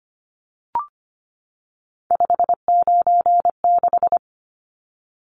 Morse code, heard as a keyed tone, about 25 wpm, 690 Hz. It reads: 596
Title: Numbers - 3-Digit 25wpm